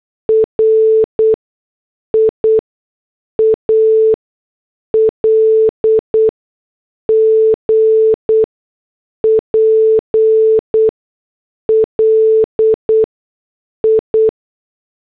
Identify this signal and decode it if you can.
Morse code, heard as a keyed tone, about 8 wpm, spaced as for 6 wpm, 440 Hz. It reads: RIAL GPL I